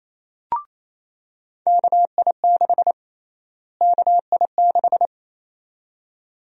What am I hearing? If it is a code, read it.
Morse code, heard as a keyed tone, about 28 wpm, 700 Hz. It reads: KI6 KI6
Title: USA Call Sign Prefixes - Repeated 2x - 28wpm